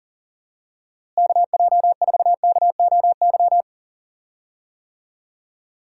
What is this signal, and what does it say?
Morse code, heard as a keyed tone, about 40 wpm, 700 Hz. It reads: KJ4KOY
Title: USA Call Signs - 6 Characters - Repeated 3x - 40wpm